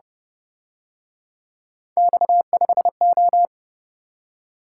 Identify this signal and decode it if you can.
Morse code, heard as a keyed tone, about 29 wpm, 700 Hz. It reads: X5O